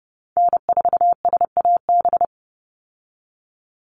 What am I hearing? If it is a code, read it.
Morse code, heard as a keyed tone, about 30 wpm, 700 Hz. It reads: N4SAB
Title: USA Call Signs 30wpm